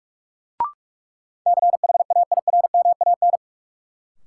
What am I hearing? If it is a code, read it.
Morse code, heard as a keyed tone, about 46 wpm, 700 Hz. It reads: CHAIRMAN
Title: Sets of 3 Words - New Word Drill - 45wpm